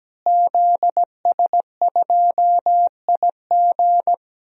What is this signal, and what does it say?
Morse code, heard as a keyed tone, about 17 wpm, 700 Hz. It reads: ZS2IG
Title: DXCC-100 Call Signs - 5 Characters - Repeated 3x - 17wpm